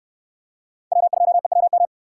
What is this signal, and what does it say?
Morse code, heard as a keyed tone, about 66 wpm, 700 Hz. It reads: W2EWM